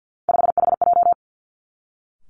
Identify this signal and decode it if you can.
Morse code, heard as a keyed tone, about 50 wpm, 700 Hz. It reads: 5HJ
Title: Three Challenging Letter-Number Combinations 50wpm